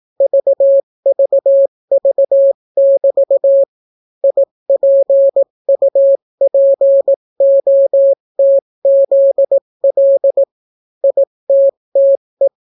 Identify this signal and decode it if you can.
Morse code, heard as a keyed tone, about 18 wpm, 550 Hz. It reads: VVV<BT> IPUPOTZL ITTE